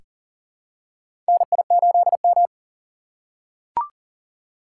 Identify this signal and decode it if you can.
Morse code, heard as a keyed tone, about 40 wpm, 700 Hz. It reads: NI8M
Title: USA Call Signs - 4 Characters - 3x Character Spacing - 40wpm